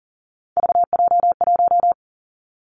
Morse code, heard as a keyed tone, about 40 wpm, 700 Hz. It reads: VJ1
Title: Three Challenging Letter-Number Combinations 40wpm